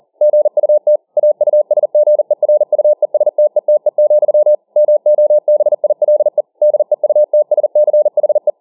Morse code, heard as a keyed tone, about 40 wpm, 600 Hz. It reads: GUT AUSGERUESTETE, MOBILE DEUTSCHE